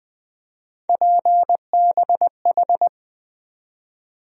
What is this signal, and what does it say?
Morse code, heard as a keyed tone, about 20 wpm, 700 Hz. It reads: PBH
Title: USA Call Sign Postfixes - 3 Letters 20wpm